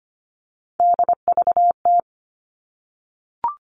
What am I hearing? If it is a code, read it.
Morse code, heard as a keyed tone, about 25 wpm, 700 Hz. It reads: DVT
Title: All Cut Numbers - 3-Digit 25wpm